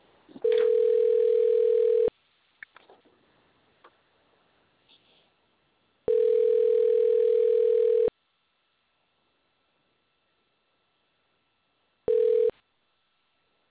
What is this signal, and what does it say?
Ringback tone (North America), 440 + 480 Hz.